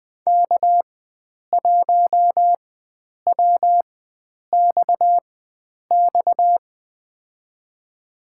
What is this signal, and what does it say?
Morse code, heard as a keyed tone, about 21 wpm, spaced as for 9 wpm, 700 Hz. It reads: K1WXX
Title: USA Call Signs - 4x Character Spacing - 20wpm